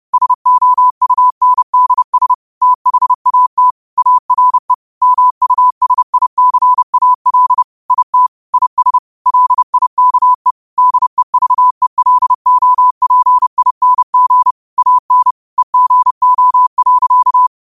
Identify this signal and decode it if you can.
Morse code, heard as a keyed tone, about 30 wpm, 1000 Hz. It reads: SOUNDS THAT ARE MUSICAL IT IS LIKE DEVELOPING AN EGO.
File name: qotdfiles / 30wpm